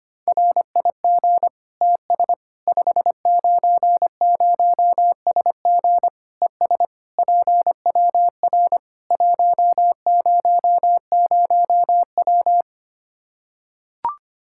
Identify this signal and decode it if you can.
Morse code, heard as a keyed tone, about 25 wpm, 700 Hz. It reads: RIG TS 590SG ES PWR 100W
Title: Practice QSO - Exchange 2 - Info - 25wpm